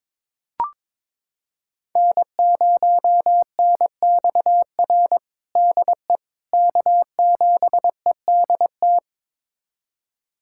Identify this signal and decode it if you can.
Morse code, heard as a keyed tone, about 22 wpm, 700 Hz. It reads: N0NXR DE K7EDT
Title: USA Callsign DE Callsign - 5 Characters - 22wpm